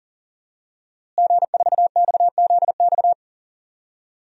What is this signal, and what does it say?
Morse code, heard as a keyed tone, about 39 wpm, 700 Hz. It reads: G4XZX
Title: DXCC-100 Call Signs - 5 Characters - Repeated 3x - 40wpm